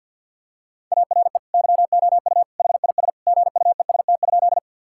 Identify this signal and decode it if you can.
Morse code, heard as a keyed tone, about 50 wpm, 700 Hz. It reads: ARE YOU HIS GUEST?
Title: Sentences from Top 1600 Words 50wpm